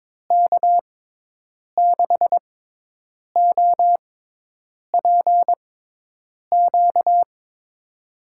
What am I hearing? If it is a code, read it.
Morse code, heard as a keyed tone, about 22 wpm, spaced as for 8 wpm, 700 Hz. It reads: K6OPQ
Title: USA Call Signs - 6x Character Spacing - 22wpm